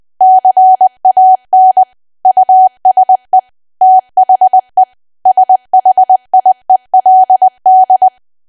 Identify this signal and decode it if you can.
Morse code, heard as a keyed tone, about 20 wpm, 750 Hz. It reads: CAN USE THE SHIELD
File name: cwa / W1AW-Level3 / 20WPM